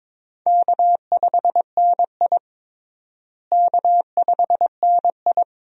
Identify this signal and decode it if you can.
Morse code, heard as a keyed tone, about 22 wpm, 700 Hz. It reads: K5NI K5NI
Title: USA Call Signs - 4 Characters - Repeated 3x - 22wpm